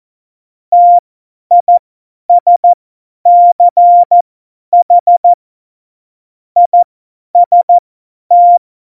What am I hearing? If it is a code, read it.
Morse code, heard as a keyed tone, about 14 wpm, spaced as for 10 wpm, 700 Hz. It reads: TISCH IST
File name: Morsekurs / Witze / Witz 23. Juli 2019, 50 F70